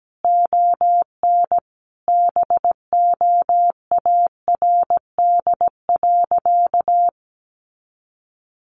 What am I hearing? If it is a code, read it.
Morse code, heard as a keyed tone, about 17 wpm, 700 Hz. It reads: ON BOARD.
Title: Sentences from Top 1400 Words 17wpm